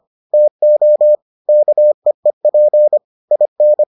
Morse code, heard as a keyed tone, about 25 wpm, 600 Hz. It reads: TO KEEP IN